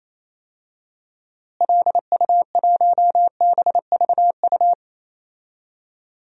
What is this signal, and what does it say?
Morse code, heard as a keyed tone, about 28 wpm, 700 Hz. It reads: LU1BVU
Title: DXCC-100 Call Signs - 6 Characters - Repeated 3x - 28wpm